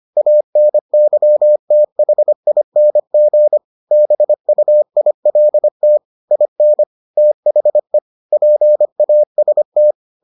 Morse code, heard as a keyed tone, about 25 wpm, 600 Hz. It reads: ANYTHING BUILT IN THE PAST